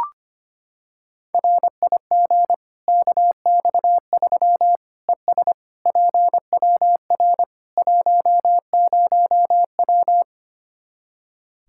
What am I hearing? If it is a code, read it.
Morse code, heard as a keyed tone, about 25 wpm, 700 Hz. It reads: RIG KX3 ES PWR 10W
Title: Practice QSO - Exchange 2 - Info - 25wpm